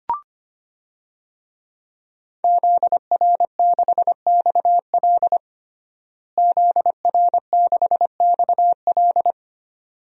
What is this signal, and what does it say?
Morse code, heard as a keyed tone, about 25 wpm, 700 Hz. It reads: ZR6XL ZR6XL
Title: DXCC-100 Call Signs - 5 Characters - Repeated 2x - 25wpm